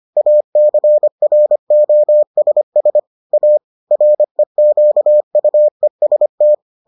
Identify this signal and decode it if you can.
Morse code, heard as a keyed tone, about 25 wpm, 600 Hz. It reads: ACROSS A REQUEST